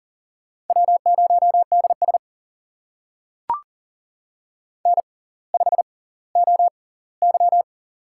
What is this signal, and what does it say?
Morse code, heard as a keyed tone, about 40 wpm, spaced as for 20 wpm, 700 Hz. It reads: W0DS N5OY
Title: USA Call Signs - 4 Characters - 6x Character Spacing - 40wpm